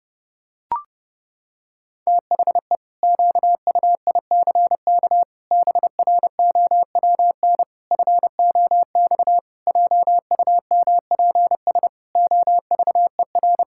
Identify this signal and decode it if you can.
Morse code, heard as a keyed tone, about 30 wpm, 700 Hz. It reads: THE QUICK BROWN FOX JUMPS OVER
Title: Pangram Challenge v2 - 30wpm